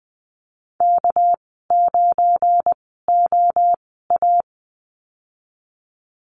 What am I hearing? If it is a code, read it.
Morse code, heard as a keyed tone, about 20 wpm, spaced as for 15 wpm, 700 Hz. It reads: K9OA
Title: 200 Lesson 32g - Letter k - Call Signs 20wpm